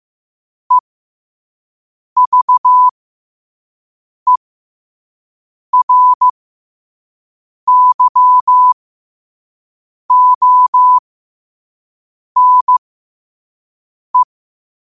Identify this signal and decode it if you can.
Morse code, heard as a keyed tone, about 15 wpm, spaced as for 5 wpm, 1000 Hz. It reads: EVERYONE
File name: qotdfiles / 5wpm